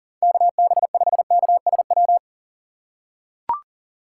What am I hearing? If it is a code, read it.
Morse code, heard as a keyed tone, about 40 wpm, 700 Hz. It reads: KB5KSW
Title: USA Call Signs - 4x Character Spacing - 40wpm